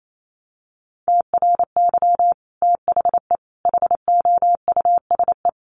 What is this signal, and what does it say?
Morse code, heard as a keyed tone, about 28 wpm, 700 Hz. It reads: TRY THE HOUSE
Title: Sentences from Top 1300 Words 28wpm